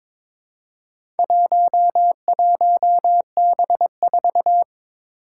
Morse code, heard as a keyed tone, about 22 wpm, 700 Hz. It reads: 11B4
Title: Four Challenging Letter-Number Combinations 22wpm